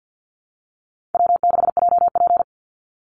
Morse code, heard as a keyed tone, about 48 wpm, 700 Hz. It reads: W6JP
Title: NA SOTA Chaser Callsigns 50wpm